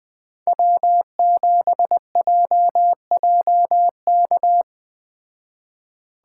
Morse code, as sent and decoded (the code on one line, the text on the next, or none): .-- --... .--- .--- -.-
W7JJK